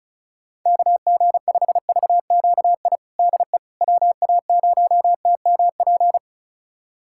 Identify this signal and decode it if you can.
Morse code, heard as a keyed tone, about 36 wpm, 700 Hz. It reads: KG5VQI DE WA0TMP
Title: USA Callsign DE Callsign 35wpm